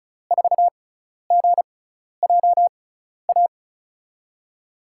Morse code, heard as a keyed tone, about 34 wpm, spaced as for 12 wpm, 700 Hz. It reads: 4GJA
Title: USA Call Signs - 5 Characters - 6x Character Spacing - 35wpm